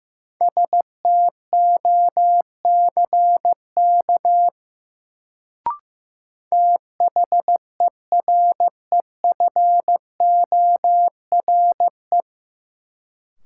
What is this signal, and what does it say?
Morse code, heard as a keyed tone, about 15 wpm, 700 Hz. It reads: STOCK THEREFORE
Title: Sentences from Top 800 Words - New Word Drill - 15wpm